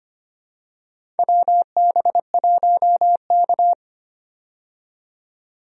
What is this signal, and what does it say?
Morse code, heard as a keyed tone, about 25 wpm, 700 Hz. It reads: WB1K
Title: USA Call Signs - 4 Characters - 25wpm